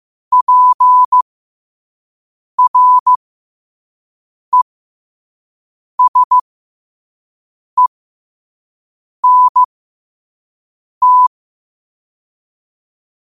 Morse code, heard as a keyed tone, about 15 wpm, spaced as for 5 wpm, 1000 Hz. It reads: PRESENT
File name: qotdfiles / 5wpm